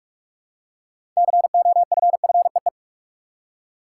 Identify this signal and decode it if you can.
Morse code, heard as a keyed tone, about 41 wpm, 700 Hz. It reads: COFFEE
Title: Six Letter Words - 45wpm